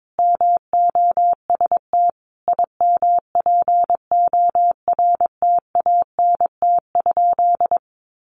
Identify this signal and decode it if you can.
Morse code, heard as a keyed tone, about 22 wpm, 700 Hz. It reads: MOST IMPORTANT?
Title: Sentences from Top 1300 Words 22wpm